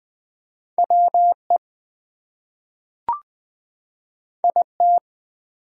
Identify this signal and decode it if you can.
Morse code, heard as a keyed tone, about 20 wpm, 700 Hz. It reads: WE IT
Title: Two Letter Words - 20wpm